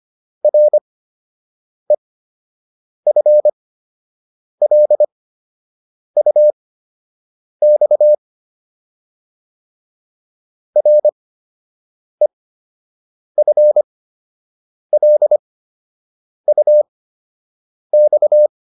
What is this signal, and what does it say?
Morse code, heard as a keyed tone, about 25 wpm, spaced as for 7 wpm, 600 Hz. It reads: REFLUX REFLUX